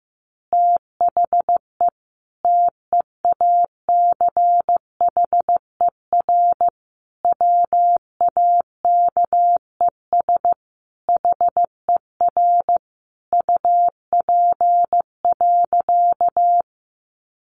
Morse code, heard as a keyed tone, about 15 wpm, 700 Hz. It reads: THE TEACHER WAKES HER UP.
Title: Easy Phrases 15wpm